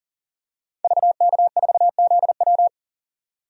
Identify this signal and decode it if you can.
Morse code, heard as a keyed tone, about 40 wpm, 700 Hz. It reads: VK4ZW
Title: DXCC-100 Call Signs - 5 Characters - 40wpm